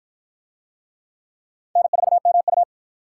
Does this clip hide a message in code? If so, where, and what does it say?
Yes: Morse code from 1.8 to 2.6 s, heard as a keyed tone, about 51 wpm, 700 Hz. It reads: N4MU